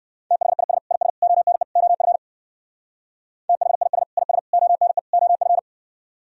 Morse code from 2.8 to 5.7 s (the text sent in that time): THIS IS ONE OF